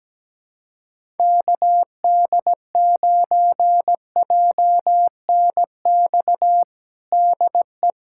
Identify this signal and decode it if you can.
Morse code, heard as a keyed tone, about 17 wpm, 700 Hz. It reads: KD9JNX DE
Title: USA Callsign DE Callsign 17wpm